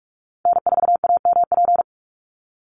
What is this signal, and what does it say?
Morse code, heard as a keyed tone, about 44 wpm, 700 Hz. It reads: N4AMP
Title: USA Call Signs 45wpm